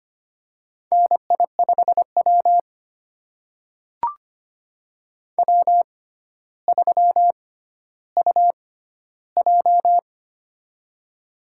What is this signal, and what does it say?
Morse code, heard as a keyed tone, about 25 wpm, spaced as for 13 wpm, 700 Hz. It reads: NI5W W3UJ
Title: USA Call Signs - 4 Characters - 6x Character Spacing - 25wpm